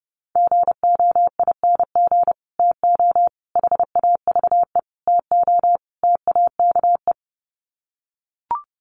Morse code, heard as a keyed tone, about 30 wpm, 700 Hz. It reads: GOING TO HAVE TO TAKE
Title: Sets of 5 Words Encore 30wpm